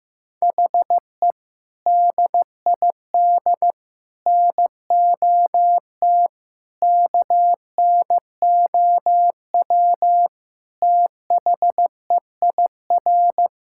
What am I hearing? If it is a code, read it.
Morse code, heard as a keyed tone, about 15 wpm, 700 Hz. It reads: HE DID NOT KNOW THEIR
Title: Sentences from Top 1000 Words - Review Part 6 - 15wpm